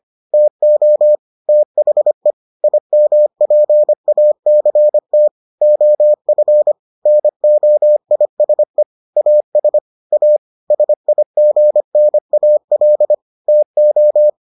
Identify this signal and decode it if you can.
Morse code, heard as a keyed tone, about 25 wpm, 600 Hz. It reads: TO THE IMPACT OF NOISE AS A SIGNAL TO